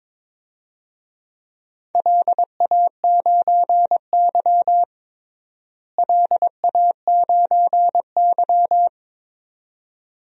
Morse code, heard as a keyed tone, about 22 wpm, 700 Hz. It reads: LA9Y LA9Y
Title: DXCC-100 Call Signs - 4 Characters - Repeated 2x - 22wpm